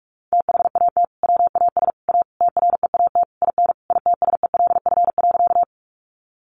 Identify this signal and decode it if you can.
Morse code, heard as a keyed tone, about 44 wpm, 700 Hz. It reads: THAT WAS A TREAT IN ITSELF.
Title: Sentences from Top 1300 Words 45wpm